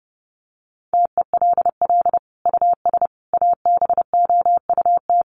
Morse code, heard as a keyed tone, about 30 wpm, 700 Hz. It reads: TELL US ABOUT
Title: Sentences from Top 1400 Words 30wpm